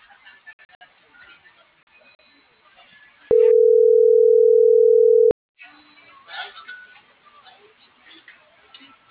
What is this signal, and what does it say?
Ringback tone (North America), 440 + 480 Hz.